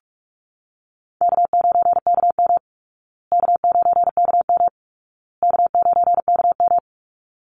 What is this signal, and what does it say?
Morse code, heard as a keyed tone, about 44 wpm, 700 Hz. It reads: K9KM K9KM K9KM
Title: Call Signs Repeated 2x - 45wpm